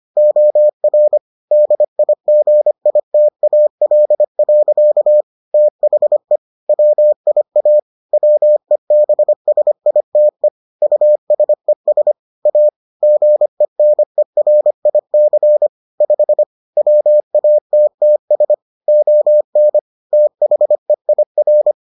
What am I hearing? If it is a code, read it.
Morse code, heard as a keyed tone, about 25 wpm, 600 Hz. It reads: OR DIGITAL. THE WIA WEBSITE USES A GENERIC 5 WATTS ON THEIR